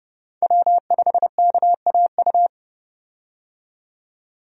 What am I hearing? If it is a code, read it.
Morse code, heard as a keyed tone, about 30 wpm, 700 Hz. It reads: W5KAU
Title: USA Call Signs - 5 Characters - 30wpm